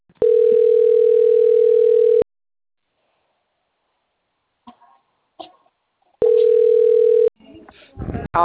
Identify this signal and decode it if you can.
Ringback tone (North America), 440 + 480 Hz, with speech.
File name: out- - - - - .81714